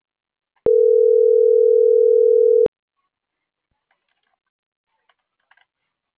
Ringback tone (North America), 440 + 480 Hz.